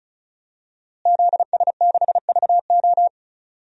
Morse code, heard as a keyed tone, about 34 wpm, 700 Hz. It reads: ZS6VO
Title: DXCC-100 Call Signs - 5 Characters - Repeated 2x - 35wpm